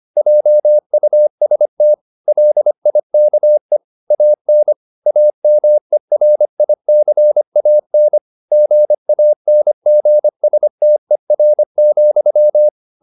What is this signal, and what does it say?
Morse code, heard as a keyed tone, about 25 wpm, 600 Hz. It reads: JUST LIKE AN AMERICAN GANGSTER,
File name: Never 0025wpm